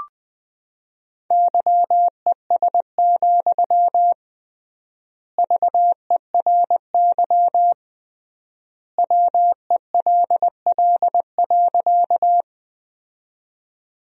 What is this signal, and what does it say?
Morse code, heard as a keyed tone, about 20 wpm, 700 Hz. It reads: YES, VERY WELL.
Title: 3x Word-Spacing - Sentences from Top 500 Words - 20wpm